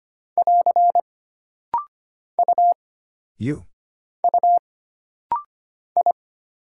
Morse code, heard as a keyed tone, about 25 wpm, 700 Hz. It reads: <AR> U U I
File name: Single Character (Letter:Number:Punctuation:Prosign) - Rapid-Fire - 25wpm